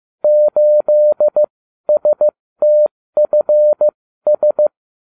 Morse code, heard as a keyed tone, about 15 wpm, spaced as for 13 wpm, 600 Hz. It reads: 8STFS